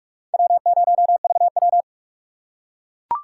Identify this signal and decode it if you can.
Morse code, heard as a keyed tone, about 41 wpm, 700 Hz. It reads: W0VW
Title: USA Call Signs - 4 Characters - 6x Character Spacing - 45wpm